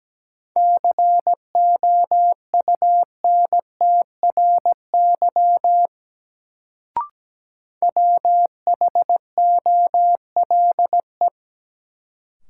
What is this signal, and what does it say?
Morse code, heard as a keyed tone, about 17 wpm, 700 Hz. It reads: COUNTRY WHOLE
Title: Sets of 2 Words - New Word Drill - 17wpm